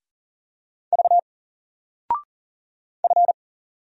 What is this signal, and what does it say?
Morse code, heard as a keyed tone, about 40 wpm, 700 Hz. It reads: V F
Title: Single Letter-Number 40wpm